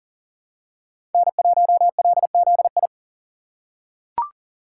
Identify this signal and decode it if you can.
Morse code, heard as a keyed tone, about 37 wpm, 700 Hz. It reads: N1LZI
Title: USA Call Signs - 5 Characters - 3x Character Spacing - 40wpm